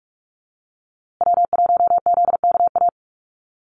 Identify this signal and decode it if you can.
Morse code, heard as a keyed tone, about 44 wpm, 700 Hz. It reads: W1ZKA
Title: USA Call Signs Repeated 3x 45wpm